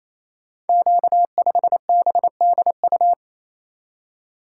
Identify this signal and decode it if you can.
Morse code, heard as a keyed tone, about 29 wpm, 700 Hz. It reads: Q5BDU
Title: Code Groups - 5 Letter:Numbers - 28wpm